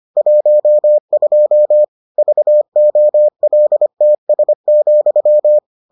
Morse code, heard as a keyed tone, about 25 wpm, 600 Hz. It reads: 12 VOLTS,